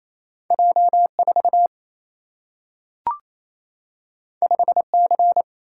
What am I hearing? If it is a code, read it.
Morse code, heard as a keyed tone, about 28 wpm, 700 Hz. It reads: J 4 5 C
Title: Single-Letter Single-Number 28wpm